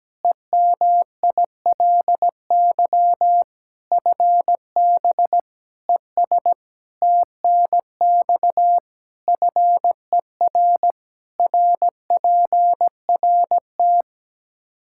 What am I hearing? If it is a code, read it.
Morse code, heard as a keyed tone, about 17 wpm, 700 Hz. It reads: EMILY FB ES TNX FER RPRT
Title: Practice QSO - Exchange 2 - Info - 17wpm